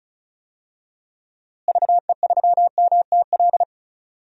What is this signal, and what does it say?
Morse code, heard as a keyed tone, about 36 wpm, 700 Hz. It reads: VE3MTL
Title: DXCC-100 Call Signs - 6 Characters - Repeated 2x - 35wpm